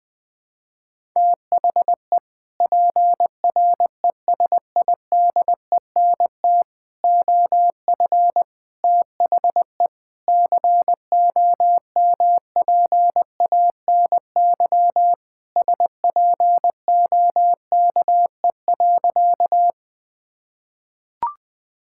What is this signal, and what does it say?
Morse code, heard as a keyed tone, about 21 wpm, 700 Hz. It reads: THE PRESIDENT OF THE COMPANY SPOKE.
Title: Sentences with 3-Word Phrases - 20wpm